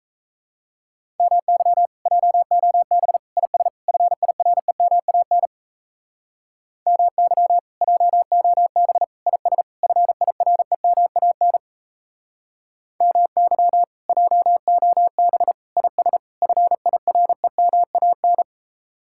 Morse code, heard as a keyed tone, about 34 wpm, 700 Hz. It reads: MY JOB IS FIREMAN MY JOB IS FIREMAN MY JOB IS FIREMAN
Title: Speed-Racing Common QSO Phrases - 28wpm